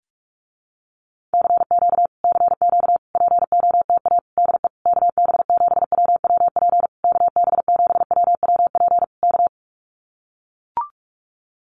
Morse code, heard as a keyed tone, about 44 wpm, 700 Hz. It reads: CQ CQ POTA DE KB7WWP KB7WWP K
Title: Common QSO Phrases Encore 45wpm